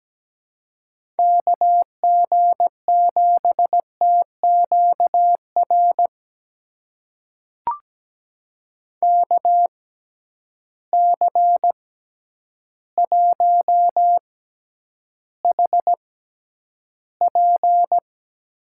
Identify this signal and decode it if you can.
Morse code, heard as a keyed tone, about 17 wpm, spaced as for 9 wpm, 700 Hz. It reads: KG7TQR KC1HP
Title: USA Call Signs - 6 Characters - 6x Character Spacing - 17wpm